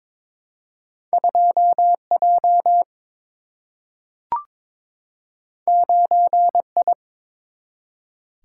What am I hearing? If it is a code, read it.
Morse code, heard as a keyed tone, about 22 wpm, 700 Hz. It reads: 2 J 9 I
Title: Single-Letter Single-Number 22wpm